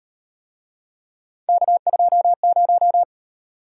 Morse code, heard as a keyed tone, about 36 wpm, 700 Hz. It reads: K20